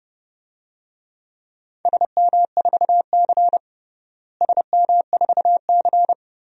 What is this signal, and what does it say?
Morse code, heard as a keyed tone, about 30 wpm, 700 Hz. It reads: SM4C SM4C
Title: DXCC-100 Call Signs - 4 Characters - Repeated 2x - 30wpm